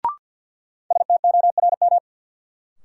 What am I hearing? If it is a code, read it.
Morse code, heard as a keyed tone, about 48 wpm, 700 Hz. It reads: STORM
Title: Sentences from Top 1500 Words - New Word Drill Alt - 50wpm